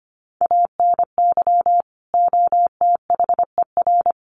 Morse code, heard as a keyed tone, about 25 wpm, 700 Hz. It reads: ANY OTHER
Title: Sentences from Top 1100 Words 25wpm